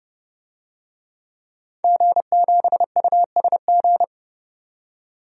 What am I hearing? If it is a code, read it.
Morse code, heard as a keyed tone, about 30 wpm, 700 Hz. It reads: G7USG
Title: DXCC-100 Call Signs - 5 Characters - 30wpm